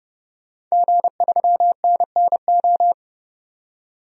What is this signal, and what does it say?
Morse code, heard as a keyed tone, about 30 wpm, 700 Hz. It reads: G3NNO
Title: DXCC-100 Call Signs - 5 Characters - 30wpm